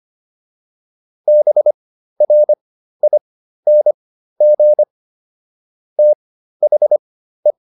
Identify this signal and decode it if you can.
Morse code, heard as a keyed tone, about 25 wpm, spaced as for 13 wpm, 600 Hz. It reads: BRING THE
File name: PR402 13